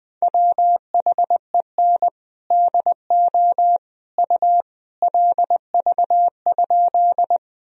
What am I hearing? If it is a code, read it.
Morse code, heard as a keyed tone, about 20 wpm, 700 Hz. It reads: WHEN DO U LV?
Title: Sentences from Top 400 Words and Abbreviations 20wpm